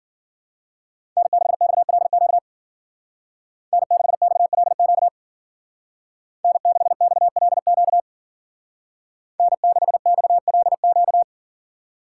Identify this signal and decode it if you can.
Morse code, heard as a keyed tone, about 49 wpm, 700 Hz. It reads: N6XLQ N6XLQ N6XLQ N6XLQ
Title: Speed Racing - USA 1x3 Call Signs 40wpm